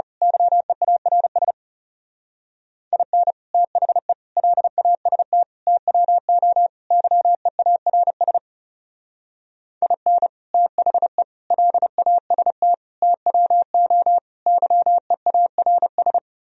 Morse code, heard as a keyed tone, about 34 wpm, 700 Hz. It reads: YEARS IN THE LAST TWO YEARS IN THE LAST TWO YEARS
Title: Speed-Racing - Sets of 5 Words 30wpm